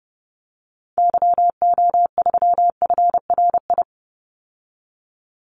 Morse code, heard as a keyed tone, about 30 wpm, 700 Hz. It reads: YO3FRI
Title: Call Signs Repeated 2x - 30wpm